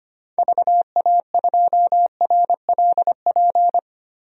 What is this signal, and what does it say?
Morse code, heard as a keyed tone, about 25 wpm, 700 Hz. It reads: VA2RLP
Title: DXCC-100 Call Signs - 6 Characters - 25wpm